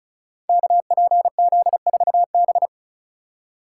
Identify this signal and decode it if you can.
Morse code, heard as a keyed tone, about 33 wpm, 700 Hz. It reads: KPZ4B
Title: Code Groups - 5 Letter:Numbers - 35wpm